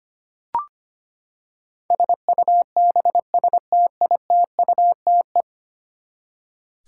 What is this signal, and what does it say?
Morse code, heard as a keyed tone, about 25 wpm, 700 Hz. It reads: SUBSTITUTE
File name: practice / 25wpm / Ten Letter Words - 25wpm